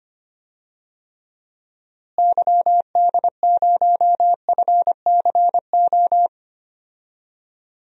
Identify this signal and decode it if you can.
Morse code, heard as a keyed tone, about 25 wpm, 700 Hz. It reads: YD0FCO
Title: DXCC-100 Call Signs - 6 Characters - Repeated 2x - 25wpm